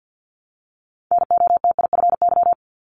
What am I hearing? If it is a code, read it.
Morse code, heard as a keyed tone, about 50 wpm, 700 Hz. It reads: NOTIFY
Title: Multi-letter Phonemes 50wpm